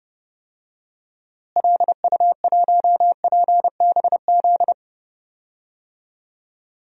Morse code, heard as a keyed tone, about 30 wpm, 700 Hz. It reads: LU1PBZ
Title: DXCC-100 Call Signs - 6 Characters - Repeated 2x - 30wpm